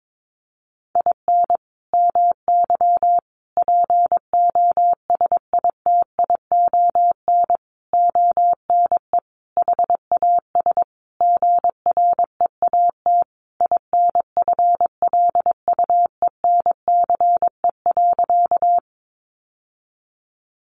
Morse code, heard as a keyed tone, about 22 wpm, 700 Hz. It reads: IN MY POSITION ONE HAS GREAT INFLUENCE.
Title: Sentences from Top 1100 Words 22wpm